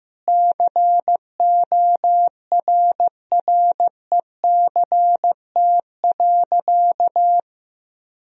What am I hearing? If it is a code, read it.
Morse code, heard as a keyed tone, about 16 wpm, 700 Hz. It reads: CORRECT.
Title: Sentences from Top 1000 Words - Review Part 3 - 15wpm